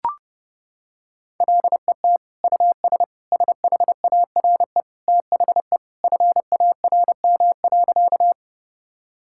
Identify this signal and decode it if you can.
Morse code, heard as a keyed tone, about 30 wpm, 700 Hz. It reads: LET US SHARE THE FARM.
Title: Sentences from Top 1000 Words - Review Part 6 - 30wpm